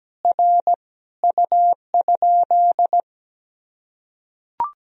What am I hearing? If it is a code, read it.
Morse code, heard as a keyed tone, about 17 wpm, 700 Hz. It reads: R U?
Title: Sentences from Top 100 Words and Abbreviations 17wpm